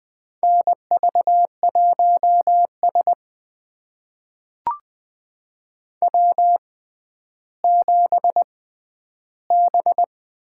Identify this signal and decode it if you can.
Morse code, heard as a keyed tone, about 20 wpm, spaced as for 11 wpm, 700 Hz. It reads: NV1S W7B
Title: USA Call Signs - 4 Characters - 6x Character Spacing - 20wpm